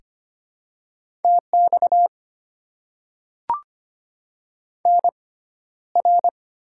Morse code, heard as a keyed tone, about 25 wpm, spaced as for 13 wpm, 700 Hz. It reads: TX NR
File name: Top 90 QSO Elements - 6x Character Spacing - 25wpm